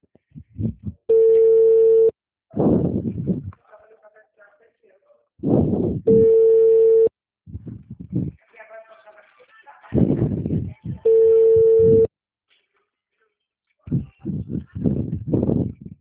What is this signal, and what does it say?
Ringback tone (China, 450 Hz), with speech.
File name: out- -2005- - - .6894